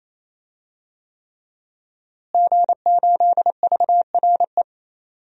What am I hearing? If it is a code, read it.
Morse code, heard as a keyed tone, about 28 wpm, 700 Hz. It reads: G8VRE
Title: DXCC-100 Call Signs - 5 Characters - Repeated 3x - 28wpm